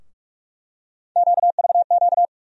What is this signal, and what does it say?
Morse code, heard as a keyed tone, about 46 wpm, 700 Hz. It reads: QVQ
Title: USA Call Sign Postfixes - 3 Letters 45wpm